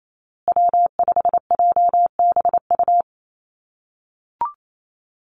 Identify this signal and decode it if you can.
Morse code, heard as a keyed tone, about 28 wpm, 700 Hz. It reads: W5JBU
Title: USA Call Signs Repeated 3x 28wpm